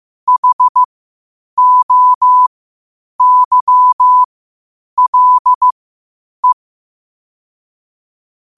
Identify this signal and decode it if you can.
Morse code, heard as a keyed tone, about 15 wpm, spaced as for 8 wpm, 1000 Hz. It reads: HOYLE
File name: qotdfiles / 8wpm